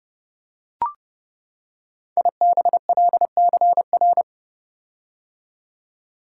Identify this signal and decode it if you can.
Morse code, heard as a keyed tone, about 30 wpm, 700 Hz. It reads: IBLCR
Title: Code Groups - 5 Letter:Numbers - 30wpm